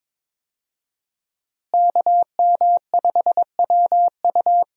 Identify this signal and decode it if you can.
Morse code, heard as a keyed tone, about 22 wpm, 700 Hz. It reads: KM5WU
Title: USA Call Signs - 5 Characters - Repeated 2x - 22wpm